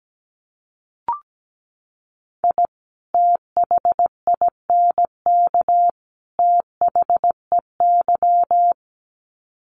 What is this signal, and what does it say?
Morse code, heard as a keyed tone, about 17 wpm, 700 Hz. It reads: I THINK THEY
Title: Sets of 3 Words 17wpm